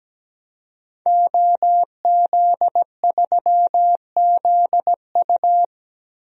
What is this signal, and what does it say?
Morse code, heard as a keyed tone, about 17 wpm, 700 Hz. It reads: OZ3ZU
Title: DXCC-100 Call Signs - 5 Characters - 17wpm